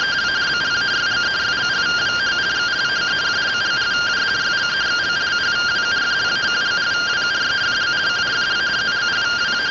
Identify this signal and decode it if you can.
Two-tone teleprinter signal, about 170 Hz shift.